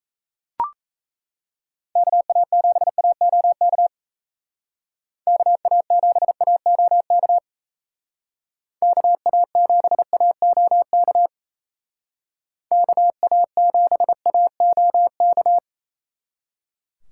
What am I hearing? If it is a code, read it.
Morse code, heard as a keyed tone, about 36 wpm, 700 Hz. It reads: KA7AOK KA7AOK KA7AOK KA7AOK